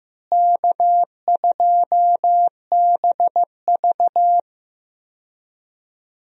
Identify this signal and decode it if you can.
Morse code, heard as a keyed tone, about 15 wpm, 700 Hz. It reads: K2BV
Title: USA Call Signs - 4 Characters - Repeated 2x - 15wpm